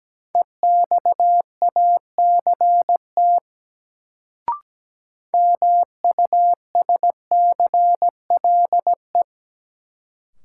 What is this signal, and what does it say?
Morse code, heard as a keyed tone, about 17 wpm, 700 Hz. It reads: EXACT MUSCLE
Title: Sentences from Top 1300 Words - New Word Drill Alt - 17wpm